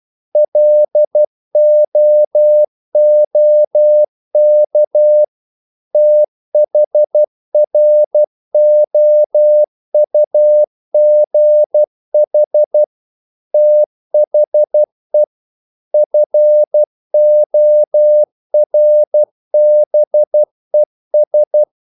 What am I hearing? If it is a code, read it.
Morse code, heard as a keyed tone, about 12 wpm, 600 Hz. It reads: LOOK THROUGH THE FORBES
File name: CW / Practice / EnglishQuotations0312